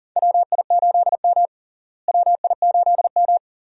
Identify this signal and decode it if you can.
Morse code, heard as a keyed tone, about 40 wpm, 700 Hz. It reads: WI8M WI8M
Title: USA Call Signs - 4 Characters - Repeated 2x - 40wpm